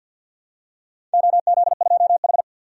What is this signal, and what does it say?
Morse code, heard as a keyed tone, about 51 wpm, 700 Hz. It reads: OZ2H
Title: DXCC-100 Call Signs - 4 Characters - 50wpm